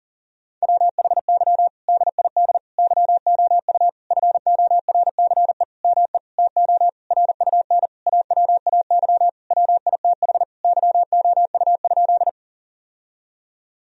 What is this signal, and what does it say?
Morse code, heard as a keyed tone, about 40 wpm, 700 Hz. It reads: WHY DID YOU FORCE ME TO RUN AWAY WITH YOU?
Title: Sentences from Top 1000 Words - Review Part 3 - 40wpm